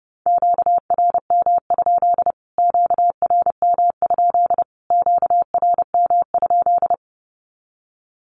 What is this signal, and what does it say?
Morse code, heard as a keyed tone, about 30 wpm, 700 Hz. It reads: QRM? QRM? QRM?
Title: Q-Codes Repeated 3x 30wpm